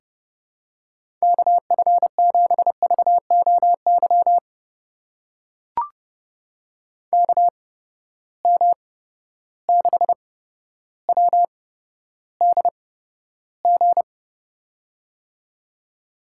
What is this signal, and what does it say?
Morse code, heard as a keyed tone, about 31 wpm, spaced as for 13 wpm, 700 Hz. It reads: KF7VOY KM6WDG
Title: USA Call Signs - 6 Characters - 8x Character Spacing - 30wpm